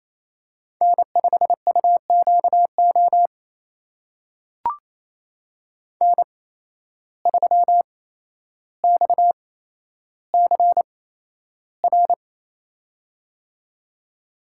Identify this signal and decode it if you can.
Morse code, heard as a keyed tone, about 28 wpm, spaced as for 12 wpm, 700 Hz. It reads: N5UQO N3XCR
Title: USA Call Signs - 5 Characters - 8x Character Spacing - 28wpm